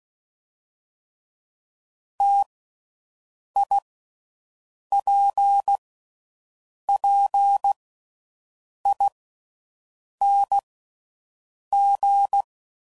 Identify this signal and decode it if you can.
Morse code, heard as a keyed tone, about 16 wpm, 800 Hz. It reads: T I P P I N G
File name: Combinations / Difficult Combos / T AOEI's